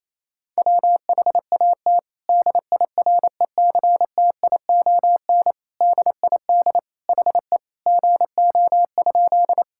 Morse code, heard as a keyed tone, about 28 wpm, 700 Hz. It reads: WHAT DIRECTION DID HE GO?
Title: Sentences from Top 1000 Words - Review Part 5 - 28wpm